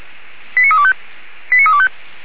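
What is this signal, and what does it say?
Five-tone selective call (ZVEI), heard as quick runs of five tones, tones of about 70 ms. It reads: 89325 89325
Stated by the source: ZVEI five-tone selective call received on an SDR, fire brigade Feuerwehr Orlamünde.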